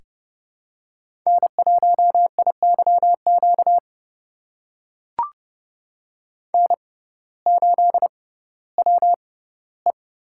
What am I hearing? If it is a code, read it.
Morse code, heard as a keyed tone, about 30 wpm, spaced as for 17 wpm, 700 Hz. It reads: N1IYQ N8WE
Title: USA Call Signs - 5 Characters - 6x Character Spacing - 30wpm